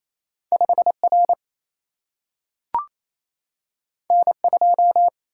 Morse code, heard as a keyed tone, about 28 wpm, 700 Hz. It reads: 5R N2
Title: Single-Letter Single-Number 28wpm